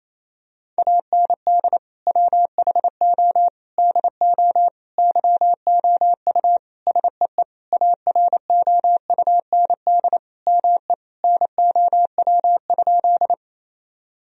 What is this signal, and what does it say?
Morse code, heard as a keyed tone, about 28 wpm, 700 Hz. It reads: AND WHO DO YOU SEE AROUND ME NOW?
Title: Sentences from Top 1000 Words - Review Part 3 - 28wpm